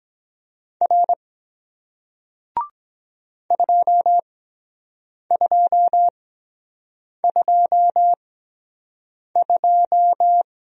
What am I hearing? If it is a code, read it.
Morse code, heard as a keyed tone, about 23 wpm, spaced as for 7 wpm, 700 Hz. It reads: R 2222